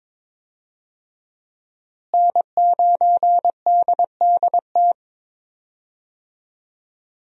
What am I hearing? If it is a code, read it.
Morse code, heard as a keyed tone, about 22 wpm, 700 Hz. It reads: N9DDT
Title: USA Call Signs - 5 Characters - Repeated 3x - 22wpm